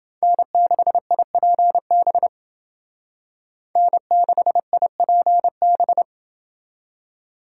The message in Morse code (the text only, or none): N6IPB N6IPB